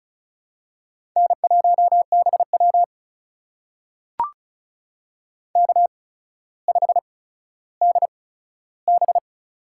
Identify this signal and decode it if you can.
Morse code, heard as a keyed tone, about 36 wpm, spaced as for 15 wpm, 700 Hz. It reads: N1BW K5DB
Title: USA Call Signs - 4 Characters - 8x Character Spacing - 35wpm